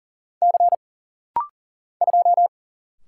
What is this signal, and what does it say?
Morse code, heard as a keyed tone, about 40 wpm, 700 Hz. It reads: C 2
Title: Single Letter-Number Alt - Rapid-Fire 40wpm